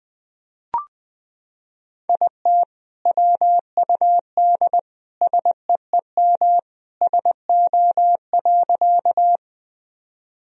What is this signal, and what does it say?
Morse code, heard as a keyed tone, about 20 wpm, 700 Hz. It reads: IT WUD SEEM SO.
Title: Sentences from Top 300 Words and Abbreviations 20wpm